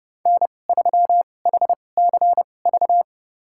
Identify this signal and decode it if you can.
Morse code, heard as a keyed tone, about 29 wpm, spaced as for 21 wpm, 700 Hz. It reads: N3HCV
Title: USA Call Signs - 5 Characters - 2x Character Spacing - 30wpm